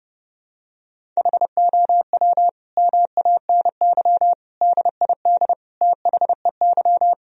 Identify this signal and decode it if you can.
Morse code, heard as a keyed tone, about 32 wpm, 700 Hz. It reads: HOW MANY DID THEY
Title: Sentences from Top 1000 Words - Review Part 1 - 30wpm